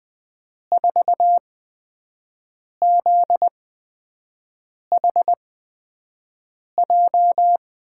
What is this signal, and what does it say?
Morse code, heard as a keyed tone, about 20 wpm, spaced as for 5 wpm, 700 Hz. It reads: 4ZHJ
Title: USA Call Signs - 6 Characters - 8x Character Spacing - 20wpm